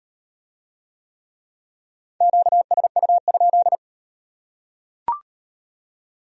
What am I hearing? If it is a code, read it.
Morse code, heard as a keyed tone, about 36 wpm, 700 Hz. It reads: QSU?